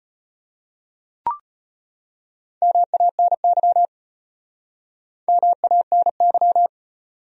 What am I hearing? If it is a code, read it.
Morse code, heard as a keyed tone, about 33 wpm, 700 Hz. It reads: MANY MANY